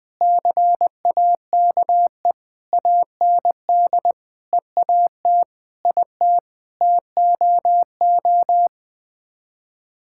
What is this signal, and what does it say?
Morse code, heard as a keyed tone, about 21 wpm, 700 Hz. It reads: CAKE AND EAT IT TOO
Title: Sets of 9 Words - 20wpm